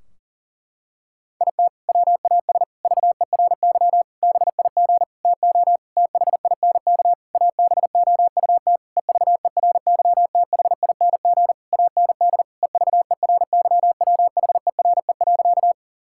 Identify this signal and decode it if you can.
Morse code, heard as a keyed tone, about 40 wpm, 700 Hz. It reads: IT WAS VERY BIG TO THINK ABOUT EVERYTHING AND EVERYWHERE.
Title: Sentences from Top 1600 Words 40wpm